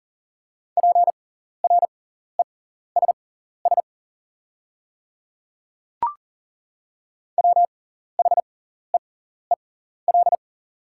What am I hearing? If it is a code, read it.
Morse code, heard as a keyed tone, about 40 wpm, spaced as for 14 wpm, 690 Hz. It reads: PRESS WHEEL
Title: 6x Character Spacing - Top 500 Words - 40wpm